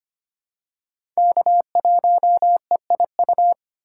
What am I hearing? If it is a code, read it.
Morse code, heard as a keyed tone, about 25 wpm, 700 Hz. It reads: K1EIU
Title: USA Call Signs - 5 Characters - 4x Character Spacing - 25wpm